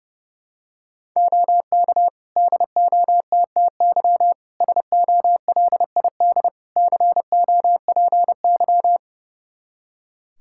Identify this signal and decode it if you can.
Morse code, heard as a keyed tone, about 30 wpm, 700 Hz. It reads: OK DOTTY SOLID COPY
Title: K1USN SST QSO Practice 30wpm